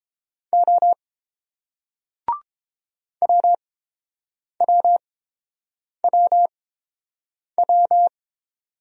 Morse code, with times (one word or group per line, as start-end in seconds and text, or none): 0.5-0.9 O
3.2-8.1 WWWW